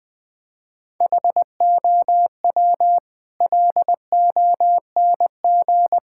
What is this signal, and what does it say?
Morse code, heard as a keyed tone, about 21 wpm, 700 Hz. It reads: HOW LONG
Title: Sentences from Top 1600 Words 20wpm